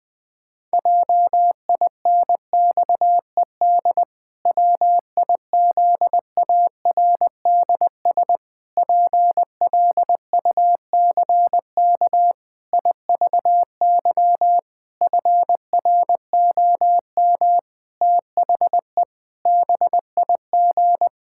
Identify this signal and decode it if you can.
Morse code, heard as a keyed tone, about 20 wpm, 700 Hz. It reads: JINXED WIZARDS PLUCK IVY FROM THE BIG
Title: Pangram Challenge v2 - 20wpm